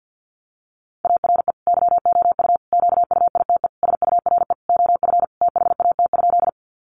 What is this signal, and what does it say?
Morse code, heard as a keyed tone, about 50 wpm, 700 Hz. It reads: ARE YOU QUITE SURE OF THAT?
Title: Sentences from Top 700 Words 50wpm